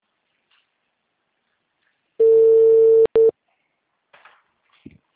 Ringback tone (China, 450 Hz).